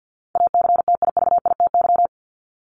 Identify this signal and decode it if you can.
Morse code, heard as a keyed tone, about 50 wpm, 700 Hz. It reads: ACTIVITY
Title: Top 501-750 Words 50wpm